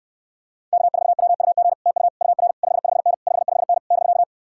Morse code, heard as a keyed tone, about 69 wpm, 700 Hz. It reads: K4WUG TU UR 55N 55N <BK>